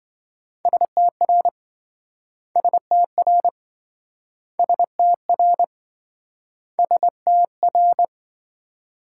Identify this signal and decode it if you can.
Morse code, heard as a keyed tone, about 27 wpm, 700 Hz. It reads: STR STR STR STR